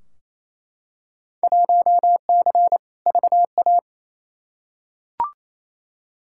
Morse code, heard as a keyed tone, about 28 wpm, 700 Hz. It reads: 1C VA
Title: Field Day Exchanges Encore - 28wpm